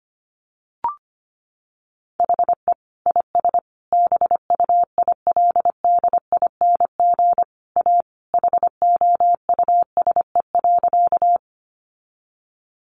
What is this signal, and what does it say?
Morse code, heard as a keyed tone, about 25 wpm, 700 Hz. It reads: HE IS BUILDING A HOUSE.
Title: Sentences from Top 700 Words 25wpm